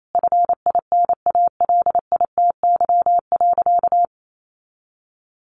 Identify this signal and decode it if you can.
Morse code, heard as a keyed tone, about 28 wpm, 700 Hz. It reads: FINALITY.
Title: A Sound Of Thunder 28wpm